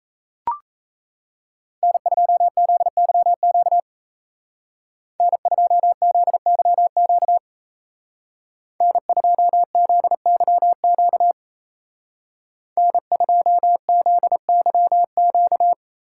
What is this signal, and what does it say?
Morse code, heard as a keyed tone, about 35 wpm, 700 Hz. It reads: N2ZYQ N2ZYQ N2ZYQ N2ZYQ
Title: Speed Racing - USA 1x3 Call Signs 28wpm